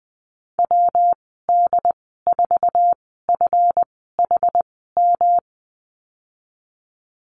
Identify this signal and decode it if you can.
Morse code, heard as a keyed tone, about 20 wpm, spaced as for 15 wpm, 700 Hz. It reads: WD4FHM
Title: 147 Lesson 24g - Letter y - Call Signs 20wpm